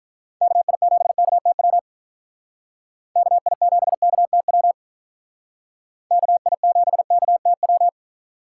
Morse code, heard as a keyed tone, about 44 wpm, 700 Hz. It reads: KI7KTW KI7KTW KI7KTW